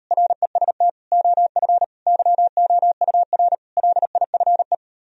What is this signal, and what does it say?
Morse code, heard as a keyed tone, about 38 wpm, 700 Hz. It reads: REST OF YOUR LIFE